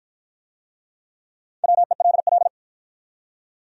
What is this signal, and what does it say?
Morse code, heard as a keyed tone, about 50 wpm, 700 Hz. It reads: WELL